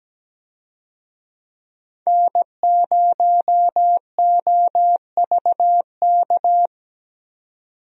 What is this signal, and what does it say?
Morse code, heard as a keyed tone, about 17 wpm, 700 Hz. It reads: N0OVK
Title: USA Call Signs - 5 Characters - 17wpm